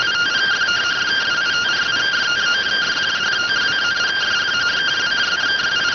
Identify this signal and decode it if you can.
Two-tone teleprinter signal, about 170 Hz shift.